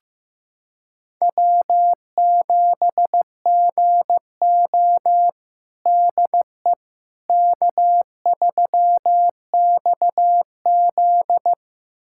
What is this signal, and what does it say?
Morse code, heard as a keyed tone, about 15 wpm, 700 Hz. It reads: W7GO DE K3XZ
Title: USA Callsign DE Callsign - 4 Characters - 15wpm